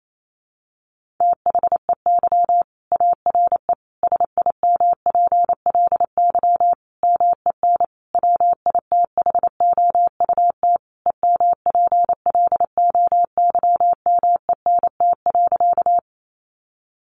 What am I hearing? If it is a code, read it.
Morse code, heard as a keyed tone, about 28 wpm, 700 Hz. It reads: THEY ARE SIMPLY MEN WITHOUT EMPLOYMENT.
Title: Sentences from Top 1500 Words 28wpm